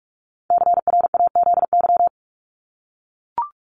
Morse code, heard as a keyed tone, about 44 wpm, 700 Hz. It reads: CRAZY
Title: Top 1501-1750 Words 45wpm